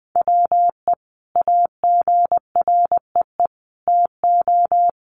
Morse code, heard as a keyed tone, about 20 wpm, 700 Hz. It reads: WE AGREE TO
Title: Sentences from Top 1100 Words 20wpm